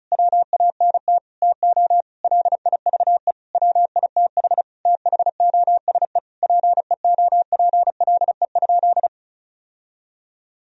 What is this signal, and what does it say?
Morse code, heard as a keyed tone, about 36 wpm, 700 Hz. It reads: WANT TO LIVE WITH THOSE PEOPLE?
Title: Sentences from Top 1000 Words - Review Part 1 - 35wpm